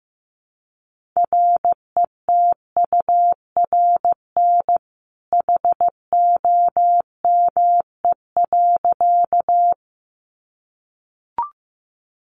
Morse code, heard as a keyed tone, about 15 wpm, 700 Hz. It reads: RETURN HOME.
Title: Sentences from Top 800 Words 15wpm